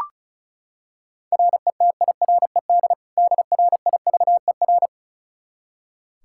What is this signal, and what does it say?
Morse code, heard as a keyed tone, about 36 wpm, 700 Hz. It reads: RETIRED DRIVER
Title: Practice QSO - Exchange 3 - More Info - 35wpm